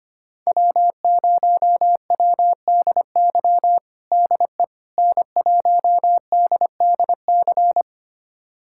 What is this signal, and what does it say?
Morse code, heard as a keyed tone, about 25 wpm, 700 Hz. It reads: W0WDY DE N1DDC
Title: USA Callsign DE Callsign - 5 Characters - 25wpm